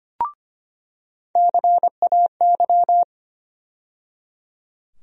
Morse code, heard as a keyed tone, about 25 wpm, 700 Hz. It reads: CAY
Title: USA Call Sign Postfixes - 3 Letters 25wpm